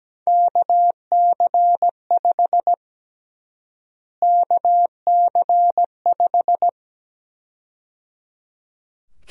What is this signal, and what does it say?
Morse code, heard as a keyed tone, about 17 wpm, 700 Hz. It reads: KC5 KC5
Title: USA Call Sign Prefixes - Repeated 2x - 17wpm